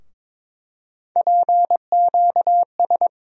Morse code, heard as a keyed tone, about 22 wpm, 700 Hz. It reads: PQS
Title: USA Call Sign Postfixes - 3 Letters 22wpm